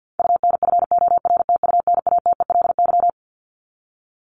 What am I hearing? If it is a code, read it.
Morse code, heard as a keyed tone, about 50 wpm, 700 Hz. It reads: UNFORTUNATELY
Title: LY Suffix - Top 100 Words with Base Words 50wpm